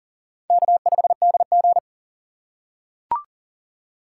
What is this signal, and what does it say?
Morse code, heard as a keyed tone, about 42 wpm, 700 Hz. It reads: K5DG
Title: USA Call Signs - 6x Character Spacing - 40wpm